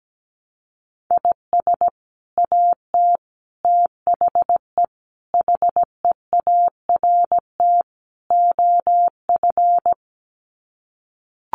Morse code, heard as a keyed tone, about 17 wpm, 700 Hz. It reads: IS AT THE HEART OF
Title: Sets of 5 Words Encore2 17wpm